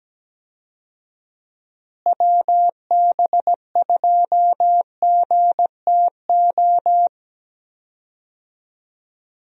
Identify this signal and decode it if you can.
Morse code, heard as a keyed tone, about 17 wpm, 700 Hz. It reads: WB2GTO
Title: USA Call Signs - 6 Characters - Repeated 3x - 17wpm